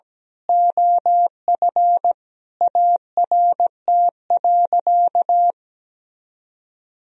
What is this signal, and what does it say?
Morse code, heard as a keyed tone, about 17 wpm, 700 Hz. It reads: OF ART.
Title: Sentences from Top 1000 Words - Review Part 3 - 17wpm